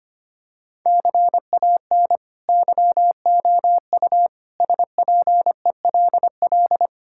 Morse code, heard as a keyed tone, about 25 wpm, 700 Hz. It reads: CAN YOU SPELL
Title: Sentences from Top 1000 Words - Review Part 5 - 25wpm